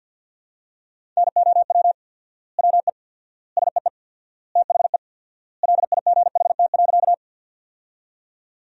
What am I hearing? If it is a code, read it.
Morse code, heard as a keyed tone, about 49 wpm, 700 Hz. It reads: NOW WE SEE THE LIGHT.
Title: 4x Word-Spacing - Sentences from Top 200 Words - 50wpm